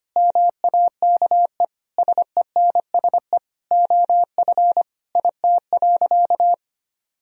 Morse code, heard as a keyed tone, about 25 wpm, 700 Hz. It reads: MAKE SENSE OF IT.
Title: Sentences from Top 1000 Words - Review Part 4 - 25wpm